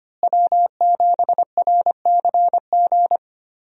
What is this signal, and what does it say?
Morse code, heard as a keyed tone, about 25 wpm, 700 Hz. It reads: W7RCG
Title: USA Call Signs - 5 Characters - Repeated 2x - 25wpm